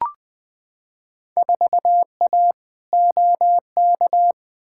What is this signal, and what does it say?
Morse code, heard as a keyed tone, about 20 wpm, 700 Hz. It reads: 4A OK
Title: Field Day Exchanges - 20wpm